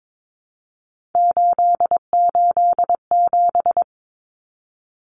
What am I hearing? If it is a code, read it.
Morse code, heard as a keyed tone, about 22 wpm, 700 Hz. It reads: 887
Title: Common Cut Numbers - 3-Digit 22wpm